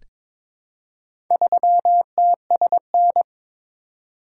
Morse code, heard as a keyed tone, about 22 wpm, 700 Hz. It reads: 3TSN
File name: practice / 22wpm / Four Letter-Number Combinations - 22wpm